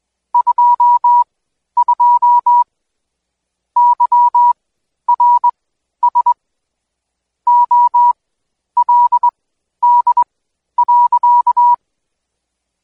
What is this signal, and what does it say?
Morse code, heard as a keyed tone, about 21 wpm, spaced as for 11 wpm, 970 Hz. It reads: 22 YRS OLD.
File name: Code Quick / AudioDownld2 / Seg120-Day 25-DayDrill 25